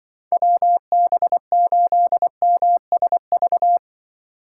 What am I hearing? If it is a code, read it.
Morse code, heard as a keyed tone, about 25 wpm, 700 Hz. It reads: WB8MSV
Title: Speed Racing - USA 2x3 Call Signs 20wpm